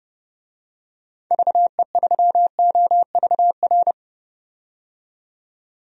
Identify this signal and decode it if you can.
Morse code, heard as a keyed tone, about 31 wpm, 700 Hz. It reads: VE3OVR
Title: DXCC-100 Call Signs - 6 Characters - 30wpm